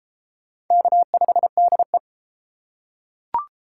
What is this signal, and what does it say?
Morse code, heard as a keyed tone, about 31 wpm, 700 Hz. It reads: K5DE